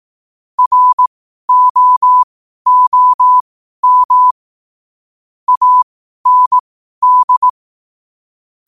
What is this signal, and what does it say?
Morse code, heard as a keyed tone, about 18 wpm, spaced as for 12 wpm, 1000 Hz. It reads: ROOM AND
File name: 12wpm